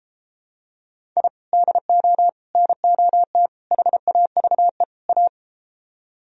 Morse code, heard as a keyed tone, about 32 wpm, 700 Hz. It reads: I DO NOT HAVE A